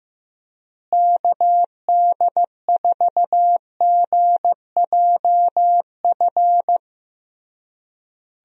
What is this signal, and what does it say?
Morse code, heard as a keyed tone, about 15 wpm, 700 Hz. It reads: KD4GJF
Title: USA Call Signs - 6 Characters - Repeated 3x - 15wpm